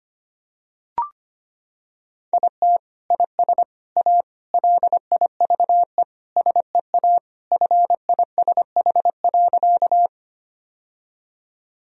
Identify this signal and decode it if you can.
Morse code, heard as a keyed tone, about 25 wpm, 700 Hz. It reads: IT IS A LIVE SEA FISH.
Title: Sentences from Top 1000 Words - Review Part 1 - 25wpm